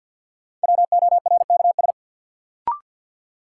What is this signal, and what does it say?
Morse code, heard as a keyed tone, about 47 wpm, 700 Hz. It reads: WORKS
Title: Sentences from Top 1100 Words - New Word Drill Alt - 50wpm